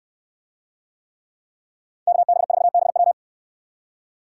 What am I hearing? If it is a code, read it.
Morse code, heard as a keyed tone, about 66 wpm, 700 Hz. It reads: KB4BW